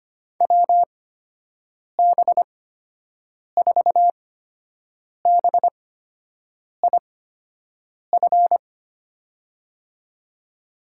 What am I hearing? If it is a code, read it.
Morse code, heard as a keyed tone, about 25 wpm, spaced as for 7 wpm, 700 Hz. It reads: WB4BIF
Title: USA Call Signs - 8x Character Spacing - 25wpm